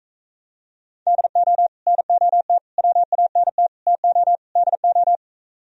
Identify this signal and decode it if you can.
Morse code, heard as a keyed tone, about 44 wpm, 700 Hz. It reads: DO NOT WANT TO DO